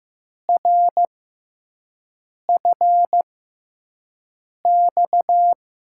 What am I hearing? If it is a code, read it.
Morse code, heard as a keyed tone, about 15 wpm, spaced as for 5 wpm, 700 Hz. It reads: RFX
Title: USA Call Signs - 5 Characters - 6x Character Spacing - 15wpm